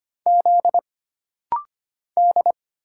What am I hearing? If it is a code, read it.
Morse code, heard as a keyed tone, about 25 wpm, 700 Hz. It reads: Z D
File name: Single Character (Letter:Number:Punctuation:Prosign) - Rapid-Fire - 25wpm